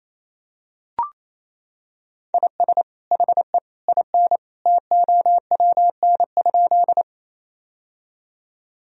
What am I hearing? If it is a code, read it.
Morse code, heard as a keyed tone, about 28 wpm, 700 Hz. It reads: IS HE IN TOWN?
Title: Sentences from Top 1000 Words - Review Part 2 - 28wpm